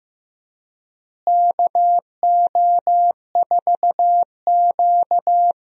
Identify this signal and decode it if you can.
Morse code, heard as a keyed tone, about 15 wpm, 700 Hz. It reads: KO4Q
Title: USA Call Signs - 4 Characters - Repeated 3x - 15wpm